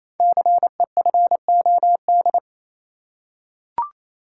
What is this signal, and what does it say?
Morse code, heard as a keyed tone, about 27 wpm, 700 Hz. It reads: CEFOD
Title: Code Groups - 5 Letters - 28wpm